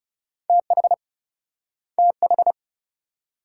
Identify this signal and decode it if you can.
Morse code, heard as a keyed tone, about 42 wpm, 700 Hz. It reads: TH TH